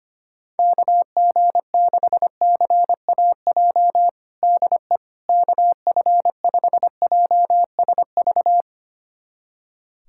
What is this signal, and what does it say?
Morse code, heard as a keyed tone, about 25 wpm, 700 Hz. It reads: KG6CAJ DE KF5JSV
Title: USA Callsign DE Callsign - 6 Characters - 25wpm